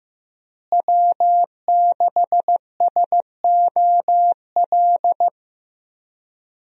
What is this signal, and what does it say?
Morse code, heard as a keyed tone, about 15 wpm, 700 Hz. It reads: W6SOL
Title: USA Call Signs - 5 Characters - Repeated 3x - 15wpm